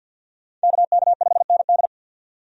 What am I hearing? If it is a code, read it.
Morse code, heard as a keyed tone, about 53 wpm, 700 Hz. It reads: KK5ND
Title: USA Call Signs - 5 Characters - 50wpm